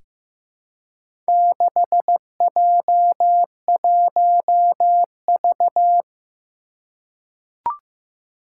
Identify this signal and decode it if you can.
Morse code, heard as a keyed tone, about 15 wpm, 700 Hz. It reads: 6J1V
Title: Four Challenging Letter-Number Combinations 15wpm